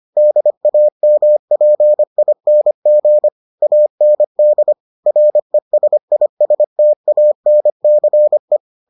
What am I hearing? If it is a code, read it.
Morse code, heard as a keyed tone, about 25 wpm, 600 Hz. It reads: DAMPING AND RESISTANCE